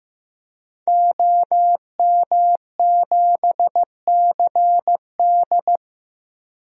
Morse code, heard as a keyed tone, about 15 wpm, 700 Hz. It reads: OM7CD
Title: DXCC-100 Call Signs - 5 Characters - Repeated 3x - 15wpm